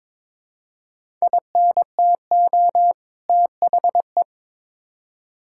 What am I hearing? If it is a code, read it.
Morse code, heard as a keyed tone, about 22 wpm, 700 Hz. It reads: INTO THE
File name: practice / 22wpm / Build A Phrase - 2-Word Phrases - 22wpm